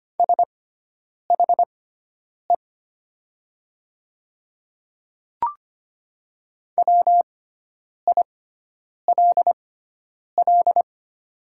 Morse code, heard as a keyed tone, about 25 wpm, spaced as for 9 wpm, 700 Hz. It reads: SHE WILL
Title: 6x Character Spacing - Top 100 Words - 25wpm